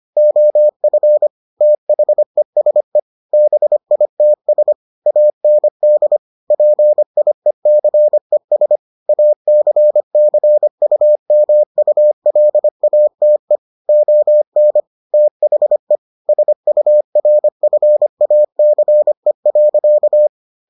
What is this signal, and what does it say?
Morse code, heard as a keyed tone, about 25 wpm, 600 Hz. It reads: OF THESE BITS AND PIECES ACCUMULATE ON THE SURFACE.